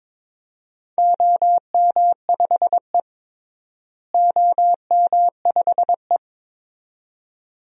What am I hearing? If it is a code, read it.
Morse code, heard as a keyed tone, about 22 wpm, 700 Hz. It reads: OM5E OM5E
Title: DXCC-100 Call Signs - 4 Characters - Repeated 2x - 22wpm